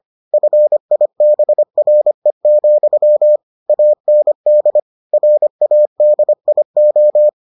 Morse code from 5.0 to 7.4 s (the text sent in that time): RADIO